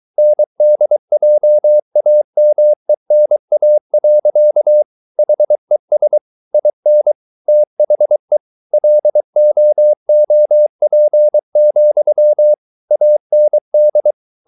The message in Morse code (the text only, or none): NDJAMENA. HES IN THE LOOP, AND